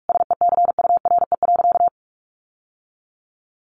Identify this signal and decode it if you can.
Morse code, heard as a keyed tone, about 44 wpm, 700 Hz. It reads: SECURE.
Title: Sentences from Top 1200 Words 45wpm